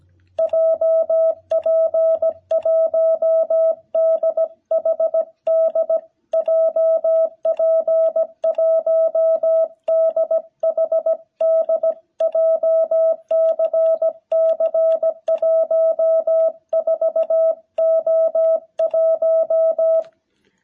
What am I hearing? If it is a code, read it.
Morse code, heard as a keyed tone, about 17 wpm, 660 Hz. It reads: JP1DHD JP1DHD JCC14O1
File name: jh8cbh / 2021cw